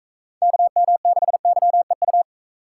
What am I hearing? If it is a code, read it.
Morse code, heard as a keyed tone, about 44 wpm, 700 Hz. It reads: KM6YEU